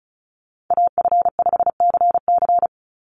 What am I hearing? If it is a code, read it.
Morse code, heard as a keyed tone, about 34 wpm, 700 Hz. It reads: AF5CC
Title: Call Signs - 35wpm